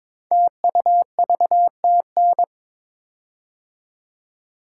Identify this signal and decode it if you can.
Morse code, heard as a keyed tone, about 22 wpm, 700 Hz. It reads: TUVTN
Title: Code Groups - 5 Letters - 22wpm